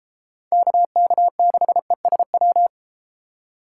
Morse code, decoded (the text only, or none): KK6ESW